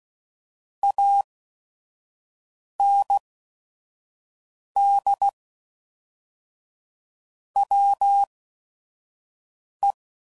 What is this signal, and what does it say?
Morse code, heard as a keyed tone, about 16 wpm, spaced as for 5 wpm, 800 Hz. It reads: AND WE
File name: CW Code Groups @5WPM